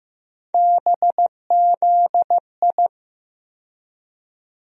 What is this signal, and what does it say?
Morse code, heard as a keyed tone, about 15 wpm, 700 Hz. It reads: BZI
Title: USA Call Sign Postfixes - 3 Letters 15wpm